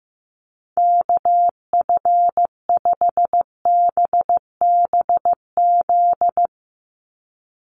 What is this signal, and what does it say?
Morse code, heard as a keyed tone, about 14 wpm, 700 Hz. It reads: KF5BBZ
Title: USA Call Signs 15wpm